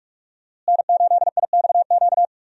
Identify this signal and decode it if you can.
Morse code, heard as a keyed tone, about 46 wpm, 700 Hz. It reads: N8IXQ
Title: USA Call Signs - 5 Characters - Repeated 2x - 45wpm